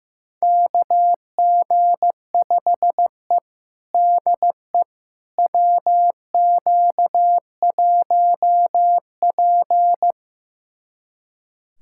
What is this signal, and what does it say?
Morse code, heard as a keyed tone, about 15 wpm, 700 Hz. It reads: KG5E DE WQ1P
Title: USA Callsign DE Callsign - 4 Characters - 15wpm